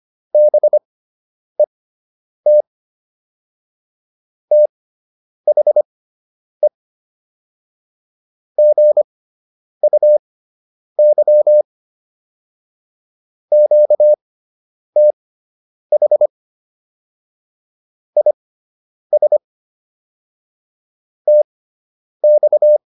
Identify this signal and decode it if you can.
Morse code, heard as a keyed tone, about 25 wpm, spaced as for 9 wpm, 600 Hz. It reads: BET THE GUY QTH IS TX